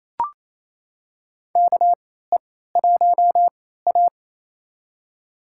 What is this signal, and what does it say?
Morse code, heard as a keyed tone, about 28 wpm, spaced as for 16 wpm, 700 Hz. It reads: KE1A
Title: USA Call Signs - 4 Characters - 3x Character Spacing - 28wpm